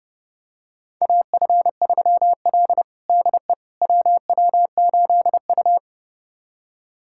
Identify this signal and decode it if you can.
Morse code, heard as a keyed tone, about 30 wpm, 700 Hz. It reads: AF3L DE WW8U
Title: USA Callsign DE Callsign - 4 Characters - 30wpm